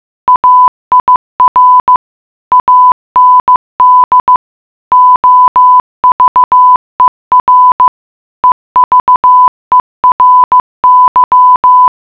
Morse code, heard as a keyed tone, about 15 wpm, 1000 Hz. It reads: AIR AND OVER EVERY
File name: mutation.15wpm